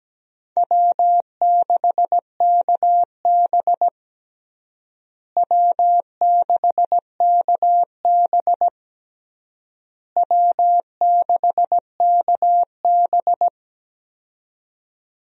Morse code, heard as a keyed tone, about 17 wpm, 700 Hz. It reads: W6KB W6KB W6KB
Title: USA Call Signs - 4 Characters - Repeated 3x - 17wpm